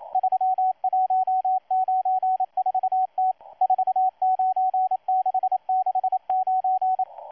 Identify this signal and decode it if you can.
Morse code, heard as a keyed tone, about 27 wpm, 750 Hz, with radio noise. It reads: 3194T 49669
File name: M12-2016-07-07-1310utc-13926khz-error-restart